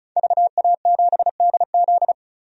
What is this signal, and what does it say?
Morse code, heard as a keyed tone, about 36 wpm, 700 Hz. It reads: VA7DZ
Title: DXCC-100 Call Signs - 5 Characters - 35wpm